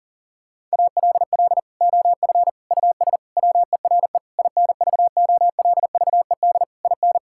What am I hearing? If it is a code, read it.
Morse code, heard as a keyed tone, about 40 wpm, 700 Hz. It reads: ALL OF US WERE INVOLVED IN
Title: Sentences from Top 1000 Words - Review Part 5 - 40wpm